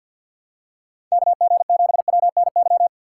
Morse code, heard as a keyed tone, about 52 wpm, 700 Hz. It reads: KG7WNY
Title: USA Call Signs - 6 Characters - 4x Character Spacing - 50wpm